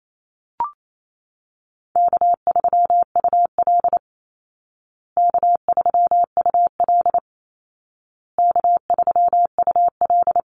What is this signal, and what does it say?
Morse code, heard as a keyed tone, about 28 wpm, 700 Hz. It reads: K3UL K3UL K3UL
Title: Call Signs Repeated 2x - 28wpm